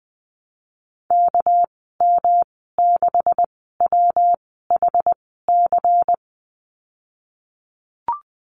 Morse code, heard as a keyed tone, about 20 wpm, spaced as for 15 wpm, 700 Hz. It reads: KM6WHC
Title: 213 Lesson 34g - Number 8 - Call Signs 20wpm